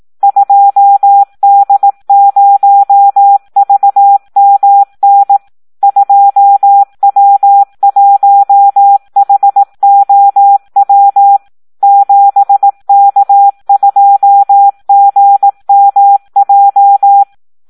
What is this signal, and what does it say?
Morse code, heard as a keyed tone, about 18 wpm, 800 Hz. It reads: 2D0VMN 2W1HOW 7K2GMJ
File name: Audio practica CW / SD 1 18